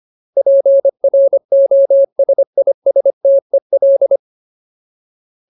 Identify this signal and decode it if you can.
Morse code, heard as a keyed tone, about 25 wpm, 550 Hz. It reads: PROSISTEL
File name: Ultimate Word List / 25WPM Ultimate